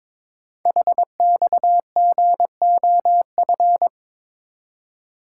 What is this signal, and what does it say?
Morse code, heard as a keyed tone, about 22 wpm, 700 Hz. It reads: HXGOF
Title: Code Groups - 5 Letters - 22wpm